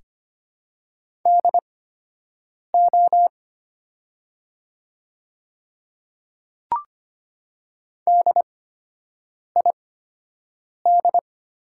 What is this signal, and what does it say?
Morse code, heard as a keyed tone, about 25 wpm, spaced as for 7 wpm, 690 Hz. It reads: DO DID
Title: 8x Character Spacing - Top 300 Words - 25wpm